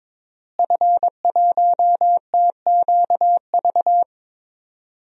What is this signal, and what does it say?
Morse code, heard as a keyed tone, about 22 wpm, 700 Hz. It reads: F1TQV
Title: DXCC-100 Call Signs - 5 Characters - Repeated 2x - 22wpm